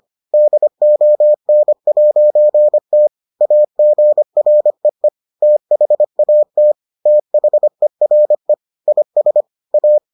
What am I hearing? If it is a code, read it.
Morse code, heard as a keyed tone, about 25 wpm, 600 Hz. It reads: DON'T AGREE THAT THERE IS A